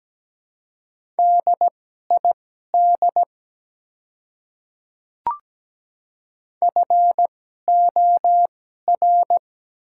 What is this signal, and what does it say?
Morse code, heard as a keyed tone, about 17 wpm, spaced as for 12 wpm, 700 Hz. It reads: DID FOR
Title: 2x Character Spacing - Top 100 Words - 17wpm